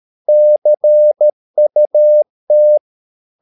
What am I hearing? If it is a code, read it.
Morse code, heard as a keyed tone, about 13 wpm, 600 Hz. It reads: CUT
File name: mp3 files / Lexiphiles - 13wpm